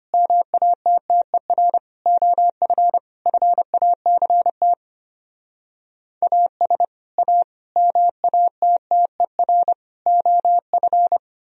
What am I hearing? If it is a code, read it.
Morse code, heard as a keyed tone, about 33 wpm, 700 Hz. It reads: MATTER OF FACT AS A MATTER OF